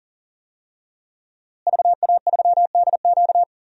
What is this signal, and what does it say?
Morse code, heard as a keyed tone, about 37 wpm, 700 Hz. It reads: VA3DQ
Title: DXCC-100 Call Signs - 5 Characters - Repeated 3x - 40wpm